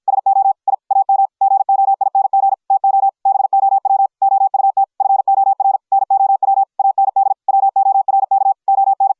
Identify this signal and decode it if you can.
Morse code, heard as a keyed tone, about 53 wpm, 750 Hz. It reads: SO I AM GOING TO BOW OUT FOR NOW AND WORK ON